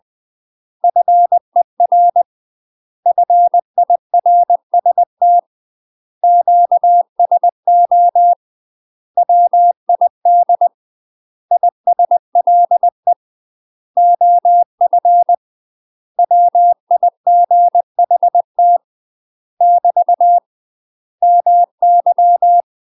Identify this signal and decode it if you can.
Morse code, heard as a keyed tone, about 20 wpm, 700 Hz. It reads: FER FIRST QSO WID ISLE OF WIGHT <BT> MY
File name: UK MASTER QSOs TRAINING / 20WPM / UK MASTER QSOs 0036